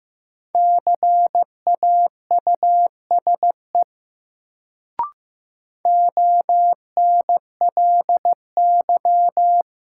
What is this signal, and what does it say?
Morse code, heard as a keyed tone, about 16 wpm, 700 Hz. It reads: CAUSE ONLY
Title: Sentences from Top 200 Words - New Word Drill - 15wpm